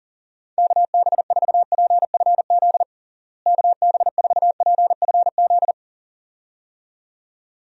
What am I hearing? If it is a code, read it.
Morse code, heard as a keyed tone, about 40 wpm, 700 Hz. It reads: KB4PFZ KB4PFZ
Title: USA Call Signs - 6 Characters - Repeated 2x - 40wpm